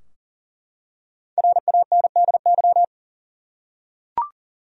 Morse code, heard as a keyed tone, about 40 wpm, 700 Hz. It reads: RANDY
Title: USA Common Names - Max 5 Letters - 40wpm